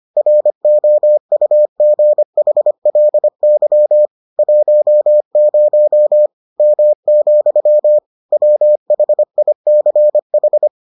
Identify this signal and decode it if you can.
Morse code, heard as a keyed tone, about 25 wpm, 600 Hz. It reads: ROUGHLY 10 M, WHICH